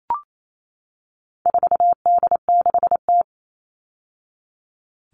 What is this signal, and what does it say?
Morse code, heard as a keyed tone, about 28 wpm, 700 Hz. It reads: 4D6T
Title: All Cut Numbers - 4-Digit 28wpm